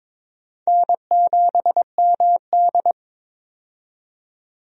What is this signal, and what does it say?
Morse code, heard as a keyed tone, about 22 wpm, 700 Hz. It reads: N7MD
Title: USA Call Signs - 4 Characters - 22wpm